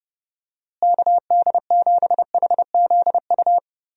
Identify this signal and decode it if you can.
Morse code, heard as a keyed tone, about 30 wpm, 700 Hz. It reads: KD7HZU
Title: USA Call Signs - 6 Characters - Repeated 3x - 30wpm